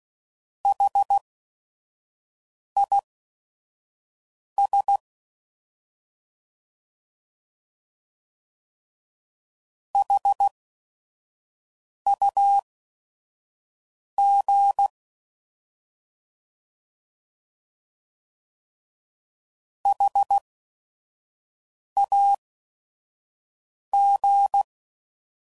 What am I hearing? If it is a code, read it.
Morse code, heard as a keyed tone, about 16 wpm, spaced as for 5 wpm, 800 Hz. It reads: HIS HUG HAG
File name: Prefixes / By First Letter / Hs